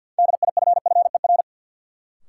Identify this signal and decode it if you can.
Morse code, heard as a keyed tone, about 52 wpm, 700 Hz. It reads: DIFFER
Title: Sentences from Top 1000 Words - Review Part 1 - New Word Drill - 50wpm